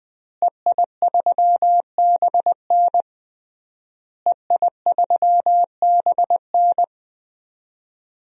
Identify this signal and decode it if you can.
Morse code, heard as a keyed tone, about 20 wpm, 700 Hz. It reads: EI3BN EI3BN
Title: DXCC-100 Call Signs - 5 Characters - Repeated 3x - 20wpm